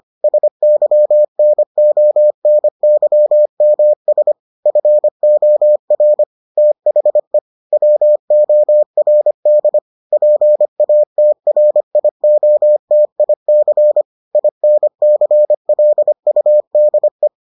Morse code, heard as a keyed tone, about 25 wpm, 600 Hz. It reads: SYNONYMS FOR THE WORD PATRIOTIC INCLUDE